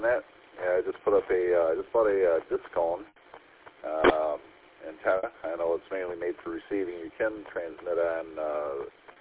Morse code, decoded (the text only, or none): none